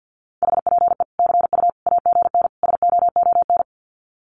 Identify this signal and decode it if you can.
Morse code, heard as a keyed tone, about 50 wpm, 700 Hz. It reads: HPE CU AGN SOON
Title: Common QSO Phrases 50wpm